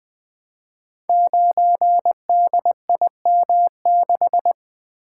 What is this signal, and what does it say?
Morse code, heard as a keyed tone, about 20 wpm, 700 Hz. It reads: 9DIM6
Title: Code Groups - 5 Letter:Numbers - 20wpm